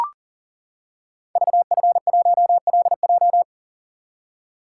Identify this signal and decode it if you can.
Morse code, heard as a keyed tone, about 38 wpm, 700 Hz. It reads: VF1LJ
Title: Code Groups - 5 Letter:Numbers - 40wpm